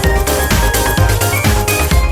There is no code.